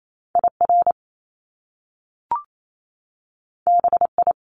Morse code, heard as a keyed tone, about 28 wpm, 700 Hz. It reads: IR BI
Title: Top Two Letter Combinations in Words 28wpm